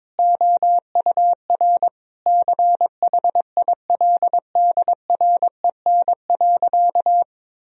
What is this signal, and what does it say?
Morse code, heard as a keyed tone, about 22 wpm, 700 Hz. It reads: OUR CHILDREN.
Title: Sentences from Top 1000 Words - Review Part 4 - 22wpm